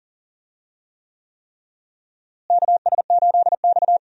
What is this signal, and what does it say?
Morse code, heard as a keyed tone, about 40 wpm, 700 Hz. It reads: KS8X
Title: USA Call Signs - 4 Characters - Repeated 3x - 40wpm